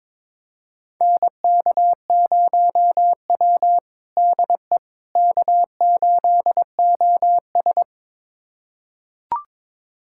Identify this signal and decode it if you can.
Morse code, heard as a keyed tone, about 22 wpm, 700 Hz. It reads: NK0W DE K8OS
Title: USA Callsign DE Callsign - 4 Characters - 22wpm